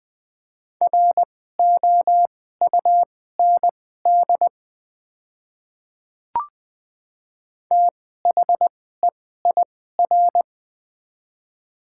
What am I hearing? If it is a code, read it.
Morse code, heard as a keyed tone, about 21 wpm, spaced as for 15 wpm, 700 Hz. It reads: ROUND THEIR
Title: 2x Character Spacing - Top 300 Words - 20wpm